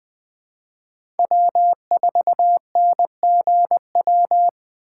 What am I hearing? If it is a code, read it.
Morse code, heard as a keyed tone, about 20 wpm, 700 Hz. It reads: W4NGW
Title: USA Call Signs - 4x Character Spacing - 20wpm